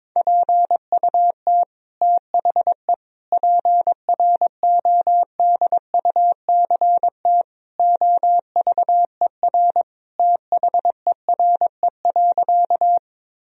Morse code, heard as a keyed tone, about 22 wpm, 700 Hz. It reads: PUT THE PRODUCT OVER THERE.
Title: Sentences from Top 1000 Words - Review Part 2 - 22wpm